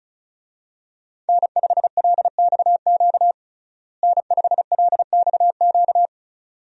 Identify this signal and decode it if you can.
Morse code, heard as a keyed tone, about 36 wpm, 700 Hz. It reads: N5LXQ N5LXQ
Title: USA Call Signs - 5 Characters - Repeated 2x - 35wpm